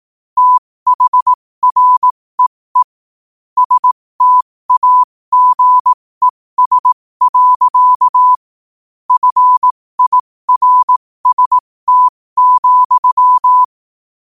Morse code, heard as a keyed tone, about 18 wpm, spaced as for 15 wpm, 1000 Hz. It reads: THREE STAGES. FIRST,
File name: qotdfiles / 15wpm